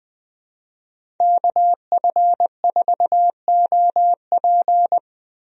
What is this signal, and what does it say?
Morse code, heard as a keyed tone, about 20 wpm, 700 Hz. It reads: KF4OP
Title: USA Call Signs - 5 Characters - Repeated 3x - 20wpm